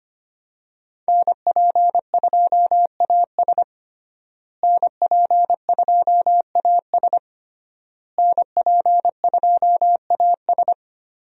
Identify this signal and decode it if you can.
Morse code, heard as a keyed tone, about 25 wpm, 700 Hz. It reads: NP2AS NP2AS NP2AS
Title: USA Call Signs - 5 Characters - Repeated 3x - 25wpm